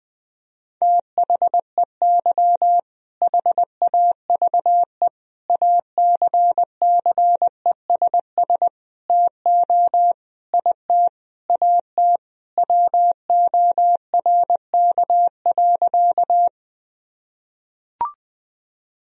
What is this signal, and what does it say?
Morse code, heard as a keyed tone, about 20 wpm, 700 Hz. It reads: THEY HAVE ACCESS TO IT AT WORK.
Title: Sentences from Top 1000 Words - Review Part 4 - 20wpm